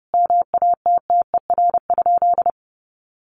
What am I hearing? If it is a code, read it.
Morse code, heard as a keyed tone, about 30 wpm, 700 Hz. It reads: MATTER?
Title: Sentences from Top 1100 Words 30wpm